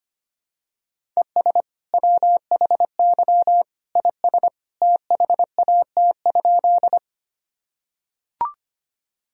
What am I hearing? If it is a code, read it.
Morse code, heard as a keyed tone, about 25 wpm, 700 Hz. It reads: ES WHY IS THAT?
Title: Sentences from Top 200 Words and Abbreviations 25wpm